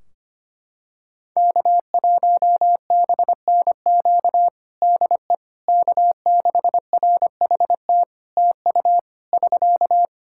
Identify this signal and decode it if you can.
Morse code, heard as a keyed tone, about 25 wpm, 700 Hz. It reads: K1BNQ DE K6RHT TU <SK>
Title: K1USN SST QSO Practice 25wpm